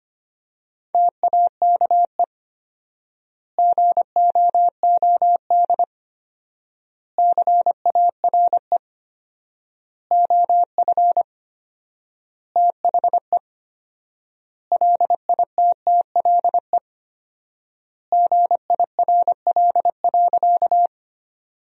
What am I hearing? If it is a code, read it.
Morse code, heard as a keyed tone, about 25 wpm, 700 Hz. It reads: TAKE GOOD CARE OF THE LITTLE GIRL.
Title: 4x Word-Spacing - Sentences from Top 400 Words - 25wpm